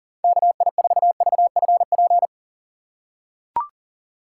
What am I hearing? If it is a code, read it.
Morse code, heard as a keyed tone, about 40 wpm, 700 Hz. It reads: KI4VFP
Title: USA Call Signs - 6 Characters - 3x Character Spacing - 40wpm